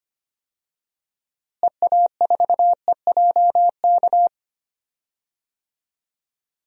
Morse code, heard as a keyed tone, about 25 wpm, 700 Hz. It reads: EA4EJK
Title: DXCC-100 Call Signs - 6 Characters - Repeated 2x - 25wpm